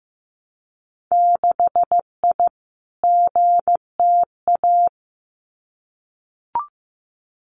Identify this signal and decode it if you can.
Morse code, heard as a keyed tone, about 15 wpm, 700 Hz. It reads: 6I GTA
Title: Winter Field Day Exchanges Encore 15wpm